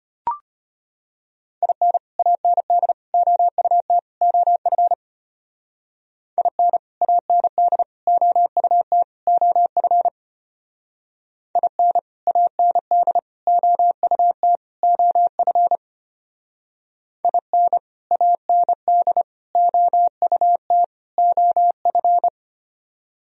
Morse code, heard as a keyed tone, about 32 wpm, 700 Hz. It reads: IN AND OUT OF IN AND OUT OF IN AND OUT OF IN AND OUT OF